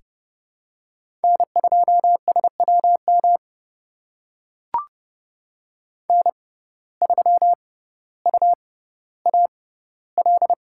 Morse code, heard as a keyed tone, about 30 wpm, spaced as for 15 wpm, 700 Hz. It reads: N2SWM N3UAL
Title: USA Call Signs - 5 Characters - 6x Character Spacing - 30wpm